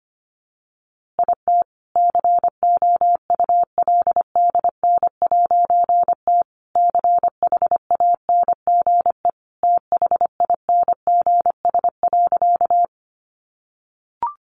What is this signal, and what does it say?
Morse code, heard as a keyed tone, about 25 wpm, 700 Hz. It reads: IT COULDN'T CHANGE THINGS.
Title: A Sound Of Thunder 25wpm